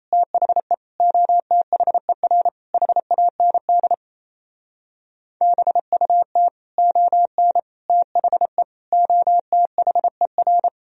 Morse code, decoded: THE OTHER HAND BUT ON THE OTHER

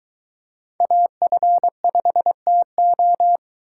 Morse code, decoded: AF5TO

